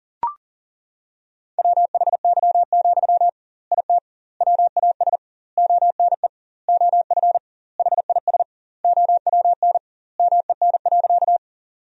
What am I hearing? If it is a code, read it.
Morse code, heard as a keyed tone, about 40 wpm, 700 Hz. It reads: WHY, IT WAS ONE OF HIS OWN MEN.
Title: 2x Word-Spacing - Sentences from Top 300 Words - 40wpm